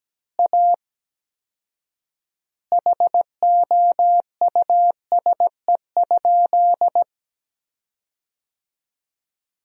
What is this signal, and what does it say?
Morse code, heard as a keyed tone, about 17 wpm, 700 Hz. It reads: A HOUSE?
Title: 4x Word-Spacing - Sentences from Top 200 Words - 17wpm